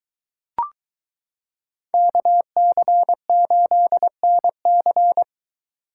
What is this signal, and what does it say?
Morse code, heard as a keyed tone, about 22 wpm, 700 Hz. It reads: KC8NC